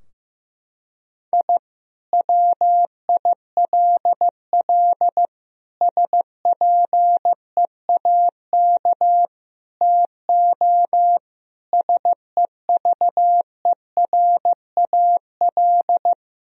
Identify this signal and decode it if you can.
Morse code, heard as a keyed tone, about 15 wpm, 700 Hz. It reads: I WILL SPEAK TO SEVERAL
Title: Sentences from Top 1000 Words - Review Part 2 - 15wpm